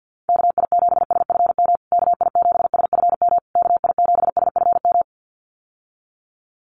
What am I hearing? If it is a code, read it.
Morse code, heard as a keyed tone, about 50 wpm, 700 Hz. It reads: KI7SFM KI7SFM KI7SFM
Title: USA Call Signs Repeated 3x 50wpm